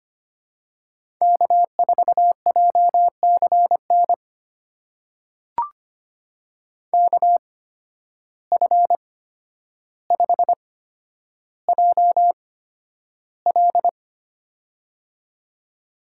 Morse code, heard as a keyed tone, about 25 wpm, spaced as for 11 wpm, 700 Hz. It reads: K4JCN KF5JL
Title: USA Call Signs - 5 Characters - 8x Character Spacing - 25wpm